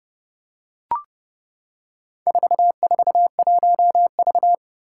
Morse code, heard as a keyed tone, about 30 wpm, 700 Hz. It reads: 441V
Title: Four Challenging Letter-Number Combinations 30wpm